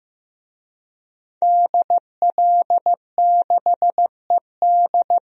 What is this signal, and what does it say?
Morse code, heard as a keyed tone, about 15 wpm, 700 Hz. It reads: DL6ED
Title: DXCC-100 Call Signs - 5 Characters - Repeated 2x - 15wpm